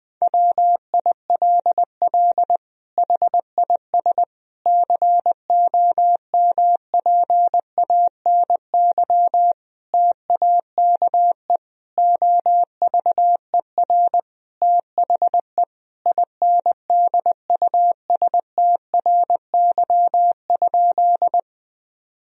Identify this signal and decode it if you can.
Morse code, heard as a keyed tone, about 20 wpm, 700 Hz. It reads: WILL HIS COMPANY TAKE OVER THE INDUSTRY?
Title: Sentences from Top 1000 Words - Review Part 4 - 20wpm